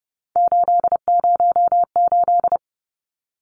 Morse code, heard as a keyed tone, about 30 wpm, 700 Hz. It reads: 808
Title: Numbers - 3-Digit 30wpm